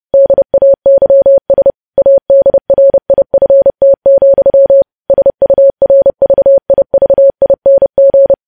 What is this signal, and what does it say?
Morse code, heard as a keyed tone, about 30 wpm, 550 Hz. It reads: DAYS ADRIFT, SURVIVING
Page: News Headlines in Morse Code at 30 WPM